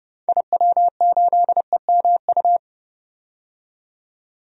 Morse code, heard as a keyed tone, about 30 wpm, 700 Hz. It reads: IW8EMU
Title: DXCC-100 Call Signs - 6 Characters - 30wpm